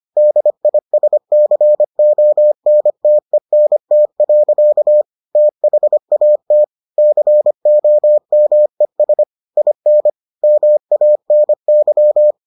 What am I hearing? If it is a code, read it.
Morse code, heard as a keyed tone, about 25 wpm, 600 Hz. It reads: DISCONTENT. THAT COMES IN MANY